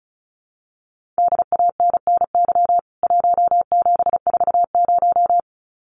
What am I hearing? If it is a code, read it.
Morse code, heard as a keyed tone, about 34 wpm, 700 Hz. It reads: DANNY 1740
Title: CWT Exchange 35wpm